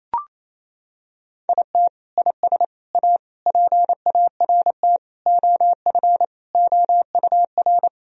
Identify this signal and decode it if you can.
Morse code, heard as a keyed tone, about 28 wpm, 700 Hz. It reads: IT IS A PART OF OUR
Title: Sentences from Top 1000 Words - Review Part 4 - 28wpm